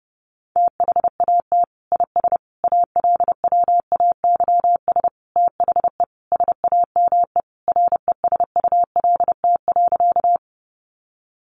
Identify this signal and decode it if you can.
Morse code, heard as a keyed tone, about 30 wpm, 700 Hz. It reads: THAT IS ALWAYS THE SAME RESULT.
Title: Sentences from Top 700 Words 30wpm